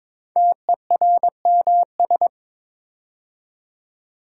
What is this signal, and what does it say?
Morse code, heard as a keyed tone, about 22 wpm, 700 Hz. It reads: TERMS
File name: 5 Letter Words - First 3 Letters Given - 22wpm